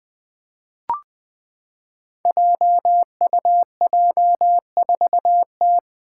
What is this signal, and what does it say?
Morse code, heard as a keyed tone, about 20 wpm, 700 Hz. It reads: JUJ4T
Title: Five Challenging Letter-Number Combinations 20wpm